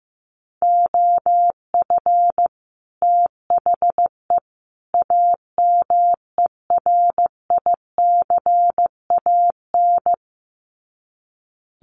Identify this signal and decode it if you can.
Morse code, heard as a keyed tone, about 15 wpm, 700 Hz. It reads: OF THE AMERICAN
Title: Sets of 3 Words 15wpm